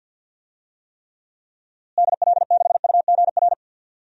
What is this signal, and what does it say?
Morse code, heard as a keyed tone, about 50 wpm, 700 Hz. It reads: DL6UGR
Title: DXCC-100 Call Signs - 6 Characters - Repeated 3x - 50wpm